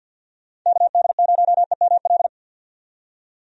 Morse code, heard as a keyed tone, about 51 wpm, 700 Hz. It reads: KD0EML
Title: USA Call Signs - 6 Characters - 50wpm